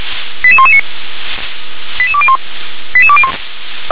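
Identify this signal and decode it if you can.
Five-tone selective call (ZVEI), heard as quick runs of five tones, tones of about 70 ms. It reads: 88119 88201 88201
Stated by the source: ZVEI five-tone selective call 88126 received via SDR.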